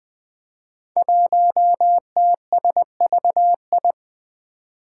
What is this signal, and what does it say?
Morse code, heard as a keyed tone, about 20 wpm, 700 Hz. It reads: 1TSVI
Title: Five Challenging Letter-Number Combinations 20wpm